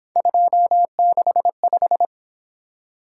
Morse code, heard as a keyed tone, about 26 wpm, 700 Hz. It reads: 265
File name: Speed-Racing - Three Letter-Number Combinations - 22wpm